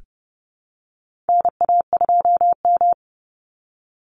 Morse code, heard as a keyed tone, about 30 wpm, 700 Hz. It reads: NA2M
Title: Call Signs Repeated 2x - 30wpm